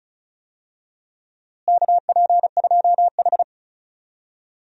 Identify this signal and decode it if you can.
Morse code, heard as a keyed tone, about 34 wpm, 700 Hz. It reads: KP2H
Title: DXCC-100 Call Signs - 4 Characters - Repeated 2x - 35wpm